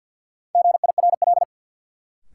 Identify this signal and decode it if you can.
Morse code, heard as a keyed tone, about 53 wpm, 700 Hz. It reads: GIRL
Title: Sentences from Top 400 Words - New Word Drill - 50wpm